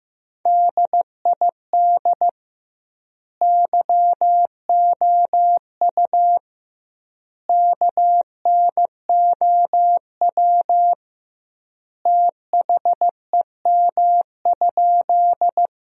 Morse code, heard as a keyed tone, about 15 wpm, 700 Hz. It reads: DID YOU KNOW THEM?
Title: 2x Word-Spacing - Sentences from Top 100 Words - 15wpm